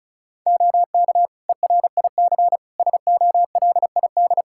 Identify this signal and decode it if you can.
Morse code, heard as a keyed tone, about 34 wpm, 700 Hz. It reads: OK ERIC SOLID
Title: Practice QSO - Exchange 3 - More Info - 35wpm